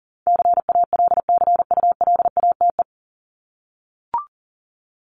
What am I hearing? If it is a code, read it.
Morse code, heard as a keyed tone, about 40 wpm, 700 Hz. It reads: CALCULATE
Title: Top 1751-2000 Words 40wpm